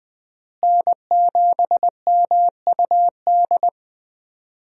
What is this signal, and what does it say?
Morse code, heard as a keyed tone, about 20 wpm, 700 Hz. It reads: N7MUD
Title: USA Call Signs - 5 Characters - Repeated 2x - 20wpm